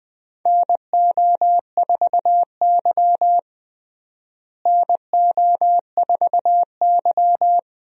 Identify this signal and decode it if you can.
Morse code, heard as a keyed tone, about 20 wpm, 700 Hz. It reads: NO4Y NO4Y
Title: USA Call Signs - 4 Characters - Repeated 2x - 20wpm